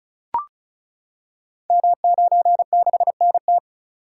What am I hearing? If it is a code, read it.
Morse code, heard as a keyed tone, about 36 wpm, 700 Hz. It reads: M96NT
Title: Code Groups - 5 Letter:Numbers - 35wpm